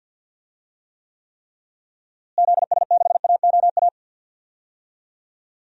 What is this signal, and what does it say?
Morse code, heard as a keyed tone, about 46 wpm, 700 Hz. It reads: ZS6AOA
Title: DXCC-100 Call Signs - 6 Characters - Repeated 3x - 50wpm